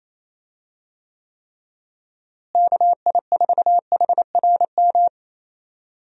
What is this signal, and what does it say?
Morse code, heard as a keyed tone, about 28 wpm, 700 Hz. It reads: KI4HRM
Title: USA Call Signs - 6 Characters - Repeated 3x - 28wpm